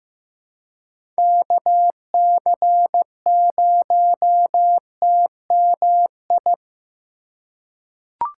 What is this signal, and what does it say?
Morse code, heard as a keyed tone, about 15 wpm, 700 Hz. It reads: KC0TMI
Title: USA Call Signs - 6 Characters - 3x Character Spacing - 15wpm